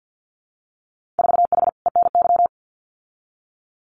Morse code, heard as a keyed tone, about 50 wpm, 700 Hz. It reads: 4H ENY
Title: Winter Field Day Exchanges Encore 50wpm